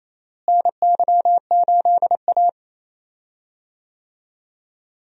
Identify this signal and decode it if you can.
Morse code, heard as a keyed tone, about 28 wpm, 700 Hz. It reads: NY8A
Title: USA Call Signs - 4 Characters - Repeated 3x - 28wpm